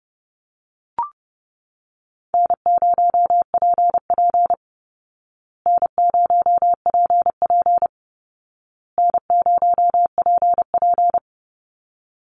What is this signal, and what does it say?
Morse code, heard as a keyed tone, about 30 wpm, 700 Hz. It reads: N0PP N0PP N0PP
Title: Call Signs Repeated 2x - 30wpm